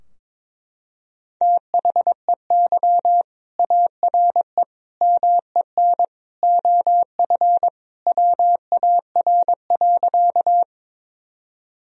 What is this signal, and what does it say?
Morse code, heard as a keyed tone, about 22 wpm, 700 Hz. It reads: THEY ARE MEN OF WAR.
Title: Sentences from Top 1000 Words - Review Part 2 - 22wpm